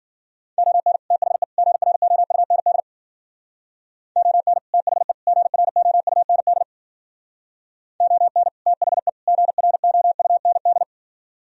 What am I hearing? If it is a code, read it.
Morse code, heard as a keyed tone, about 50 wpm, 700 Hz. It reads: ON THE GROUND ON THE GROUND ON THE GROUND